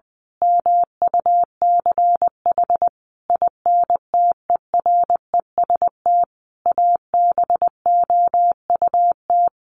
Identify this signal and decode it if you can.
Morse code, heard as a keyed tone, about 19 wpm, 700 Hz. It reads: MUCH INTEREST ABOUT
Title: Sentences from Top 1200 Words 20wpm